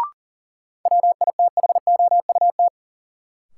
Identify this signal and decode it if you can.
Morse code, heard as a keyed tone, about 40 wpm, 700 Hz. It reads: WITHOUT
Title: Sentences from Top 600 Words - New Word Drill - 40wpm